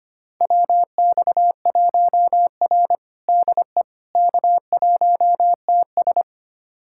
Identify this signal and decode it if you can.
Morse code, heard as a keyed tone, about 25 wpm, 700 Hz. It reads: WX1R DE K1TS
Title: USA Callsign DE Callsign - 4 Characters - 25wpm